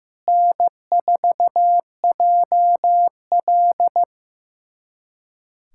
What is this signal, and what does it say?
Morse code, heard as a keyed tone, about 15 wpm, 700 Hz. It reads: N4JL
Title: USA Callsign DE Callsign - 4 Characters - 15wpm